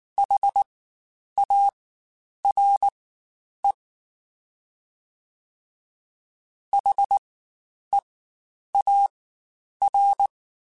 Morse code, heard as a keyed tone, about 19 wpm, spaced as for 9 wpm, 800 Hz. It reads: HARE HEAR
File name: Spaced Alphas / Spaced Alphas mp3s / Hs sp @18